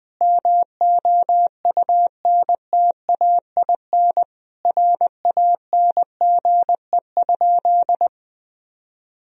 Morse code, heard as a keyed tone, about 20 wpm, 700 Hz. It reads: MOUNTAIN RANGE?
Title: Sentences from Top 1000 Words - Review Part 4 - 20wpm